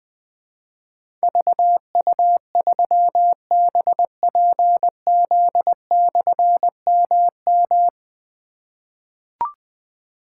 Morse code, heard as a keyed tone, about 20 wpm, 700 Hz. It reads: VU3BPZ/MM
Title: Difficult Call Signs 20wpm